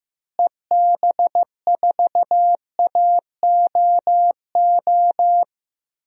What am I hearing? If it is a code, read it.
Morse code, heard as a keyed tone, about 15 wpm, 700 Hz. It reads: EB4AOO
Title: DXCC-100 Call Signs - 6 Characters - Repeated 3x - 15wpm